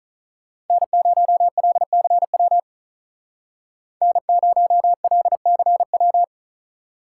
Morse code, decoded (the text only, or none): N0LCW N0LCW